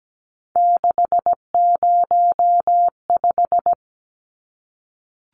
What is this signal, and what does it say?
Morse code, heard as a keyed tone, about 17 wpm, 700 Hz. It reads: 605
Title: Numbers - 3-Digit 17wpm